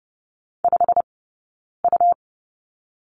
Morse code, heard as a keyed tone, about 30 wpm, spaced as for 9 wpm, 690 Hz. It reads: 5U
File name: Lesson 18d - Letter m - Two Characters 30wpm